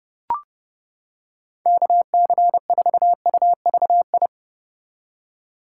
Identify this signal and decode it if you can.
Morse code, heard as a keyed tone, about 30 wpm, 700 Hz. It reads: KC4UVI